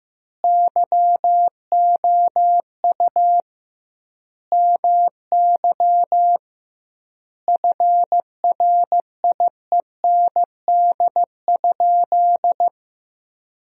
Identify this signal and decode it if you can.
Morse code, heard as a keyed tone, about 15 wpm, 700 Hz. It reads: YOU MY FRIEND?
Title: 2x Word-Spacing - Sentences from Top 300 Words - 15wpm